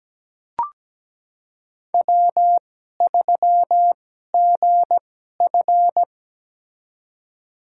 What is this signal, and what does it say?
Morse code, heard as a keyed tone, about 17 wpm, spaced as for 12 wpm, 700 Hz. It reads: W3GF
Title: USA Call Signs - 4 Characters - 2x Character Spacing - 17wpm